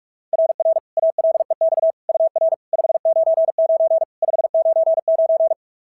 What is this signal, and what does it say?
Morse code, heard as a keyed tone, about 46 wpm, 690 Hz. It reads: RR ALEX UR 599 599
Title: SOTA QSO Practice 45wpm